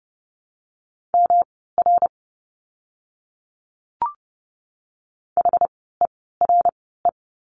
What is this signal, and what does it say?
Morse code, heard as a keyed tone, about 30 wpm, spaced as for 17 wpm, 700 Hz. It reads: MR HERE
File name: course / 30:10wpm / 115 Lesson 19f - Letter w - Words 30wpm